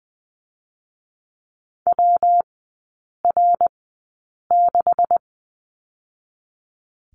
Morse code, heard as a keyed tone, about 20 wpm, spaced as for 8 wpm, 700 Hz. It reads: WR6
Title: 127 Lesson 21e - Number 6 - Three Characters 20wpm